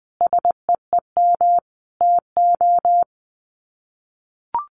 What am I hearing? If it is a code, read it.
Morse code, heard as a keyed tone, about 20 wpm, 700 Hz. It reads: SEEM TO
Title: Sets of 5 Words Encore2 20wpm